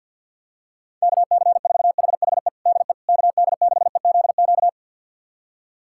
Morse code, heard as a keyed tone, about 47 wpm, 690 Hz. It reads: KC4HHE DE KD6EZQ
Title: USA Callsign DE Callsign - 6 Characters - 50wpm